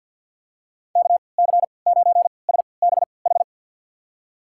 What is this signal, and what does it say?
Morse code, heard as a keyed tone, about 50 wpm, spaced as for 28 wpm, 700 Hz. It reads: KC9SBH
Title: USA Call Signs - 6 Characters - 3x Character Spacing - 50wpm